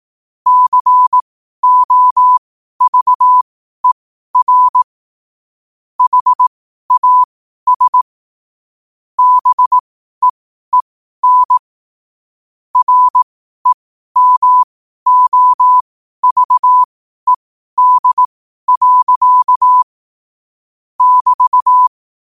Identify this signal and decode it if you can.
Morse code, heard as a keyed tone, about 18 wpm, spaced as for 12 wpm, 1000 Hz. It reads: COVER HAS BEEN REMOVED. <BT>
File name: qotdfiles / 12wpm